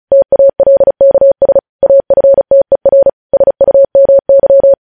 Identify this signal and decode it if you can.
Morse code, heard as a keyed tone, about 36 wpm, 550 Hz. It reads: TALKS AFTER SUMY
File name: podcast / News headlines-1x-35x35